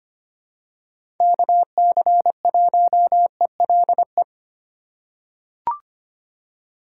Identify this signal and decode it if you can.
Morse code, heard as a keyed tone, about 25 wpm, 700 Hz. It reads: KC1ELE
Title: USA Call Signs - 6 Characters - 2x Character Spacing - 25wpm